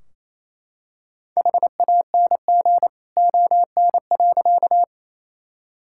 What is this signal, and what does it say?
Morse code, heard as a keyed tone, about 28 wpm, 700 Hz. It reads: HANG ON.
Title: Sentences from Top 1600 Words 28wpm